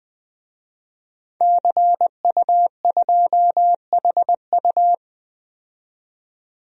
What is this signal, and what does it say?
Morse code, heard as a keyed tone, about 20 wpm, 700 Hz. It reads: CU2HU
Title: DXCC-100 Call Signs - 5 Characters - 20wpm